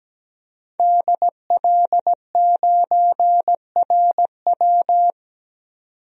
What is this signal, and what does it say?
Morse code, heard as a keyed tone, about 17 wpm, 700 Hz. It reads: DL9RW
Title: DXCC-100 Call Signs - 5 Characters - 17wpm